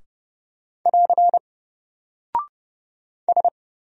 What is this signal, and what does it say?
Morse code, heard as a keyed tone, about 30 wpm, 700 Hz. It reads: <AR> S
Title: Single Character (Letter:Number:Punctuation:Prosign) 30wpm